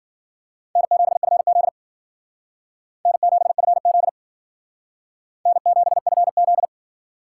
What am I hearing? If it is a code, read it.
Morse code, heard as a keyed tone, about 52 wpm, 700 Hz. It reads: N7FZ N7FZ N7FZ